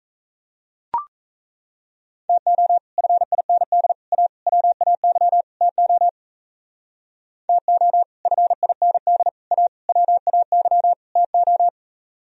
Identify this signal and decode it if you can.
Morse code, heard as a keyed tone, about 35 wpm, 700 Hz. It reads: TO FIND A WAY TO TO FIND A WAY TO
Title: Speed-Racing - Sets of 5 Words 28wpm